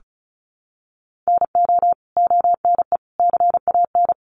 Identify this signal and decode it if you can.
Morse code, heard as a keyed tone, about 34 wpm, 700 Hz. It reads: NO ONE CAN
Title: Sentences from Top 1300 Words 35wpm